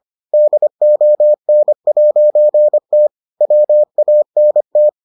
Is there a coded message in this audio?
Yes: Morse code from 0.3 to 4.9 s, heard as a keyed tone, about 25 wpm, 600 Hz. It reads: DON'T WANT